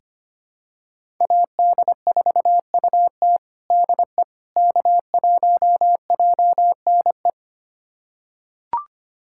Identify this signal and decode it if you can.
Morse code, heard as a keyed tone, about 25 wpm, 700 Hz. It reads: AD4UT DE K1JNE
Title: USA Callsign DE Callsign - 5 Characters - 25wpm